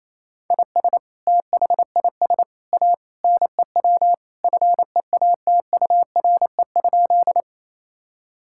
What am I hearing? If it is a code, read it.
Morse code, heard as a keyed tone, about 28 wpm, 700 Hz. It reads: IS THIS A NEW FEATURE?
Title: Sentences from Top 1000 Words - Review Part 5 - 28wpm